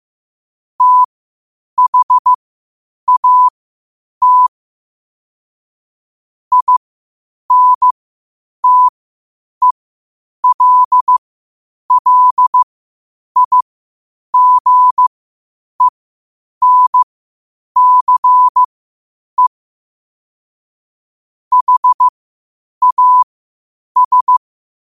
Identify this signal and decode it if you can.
Morse code, heard as a keyed tone, about 15 wpm, spaced as for 8 wpm, 1000 Hz. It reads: THAT INTELLIGENCE HAS